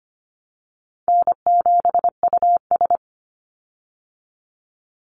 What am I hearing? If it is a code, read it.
Morse code, heard as a keyed tone, about 25 wpm, 700 Hz. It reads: N7US
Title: Call Signs Repeated 3x - 25wpm